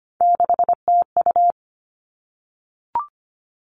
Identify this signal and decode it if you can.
Morse code, heard as a keyed tone, about 25 wpm, 700 Hz. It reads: 6TU
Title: All Cut Numbers - 3-Digit 25wpm